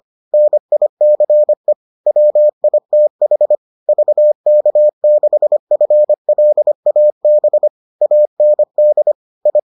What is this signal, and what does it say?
Morse code, heard as a keyed tone, about 25 wpm, 600 Hz. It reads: NICE WITH VK6FLAB AND I